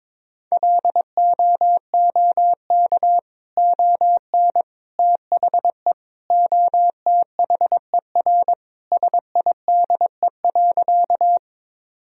Morse code, heard as a keyed tone, about 22 wpm, 700 Hz. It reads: LOOK ON THE OTHER SIDE.
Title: Sentences from Top 1000 Words - Review Part 1 - 22wpm